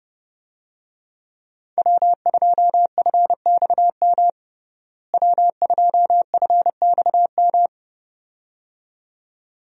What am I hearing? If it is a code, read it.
Morse code, heard as a keyed tone, about 32 wpm, 700 Hz. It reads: W2FXM W2FXM
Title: USA Call Signs - 5 Characters - Repeated 2x - 30wpm